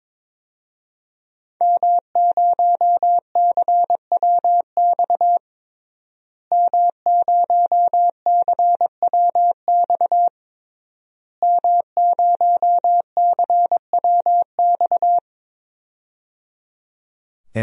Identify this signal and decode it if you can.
Morse code, heard as a keyed tone, about 22 wpm, 700 Hz. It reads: M0CWX M0CWX M0CWX
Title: DXCC-100 Call Signs - 5 Characters - Repeated 3x - 22wpm